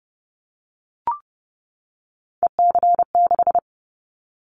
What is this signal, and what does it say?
Morse code, heard as a keyed tone, about 30 wpm, 700 Hz. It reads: EC6
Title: Callsign Prefixes for Top 100 DXCC - 30wpm